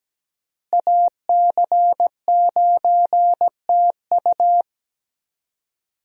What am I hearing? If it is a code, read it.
Morse code, heard as a keyed tone, about 17 wpm, 700 Hz. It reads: AC9TU
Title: USA Call Signs - 5 Characters - Repeated 2x - 17wpm